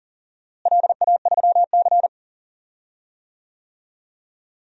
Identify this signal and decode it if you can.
Morse code, heard as a keyed tone, about 40 wpm, 700 Hz. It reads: LA3C
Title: DXCC-100 Call Signs - 4 Characters - Repeated 2x - 40wpm